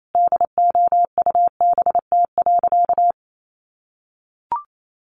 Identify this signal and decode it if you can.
Morse code, heard as a keyed tone, about 28 wpm, 700 Hz. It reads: DOUBT.
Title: Sentences from Top 1500 Words 28wpm